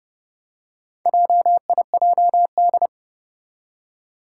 Morse code, heard as a keyed tone, about 28 wpm, 690 Hz. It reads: JIJD
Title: Four Challenging Letter-Number Combinations 30wpm